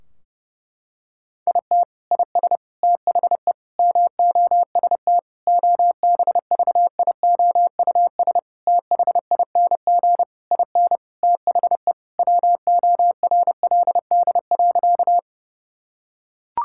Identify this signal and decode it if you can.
Morse code, heard as a keyed tone, about 32 wpm, 700 Hz. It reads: IT IS THE MOST OBVIOUS THING IN THE WORLD.
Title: Sentences from Top 1600 Words 30wpm